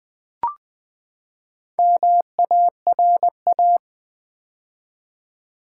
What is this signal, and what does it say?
Morse code, heard as a keyed tone, about 20 wpm, 700 Hz. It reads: MARA
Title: Build A Word - 4 Letters At A Time - 8-Letter Words - 20wpm